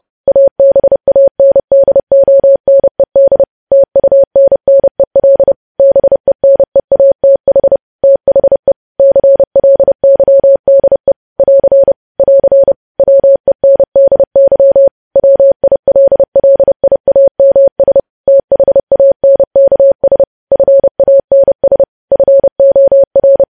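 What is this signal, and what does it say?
Morse code, heard as a keyed tone, about 30 wpm, 550 Hz. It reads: ABANDONED TUNNEL BENEATH THE CLYDE <AR> <AR> WENDY WILLIAMS THANKS FANS FOR
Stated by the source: News headlines sent at 30 wpm.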